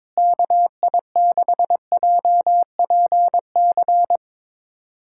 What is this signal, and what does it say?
Morse code, heard as a keyed tone, about 22 wpm, 700 Hz. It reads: KI6JPC
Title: USA Callsign DE Callsign 22wpm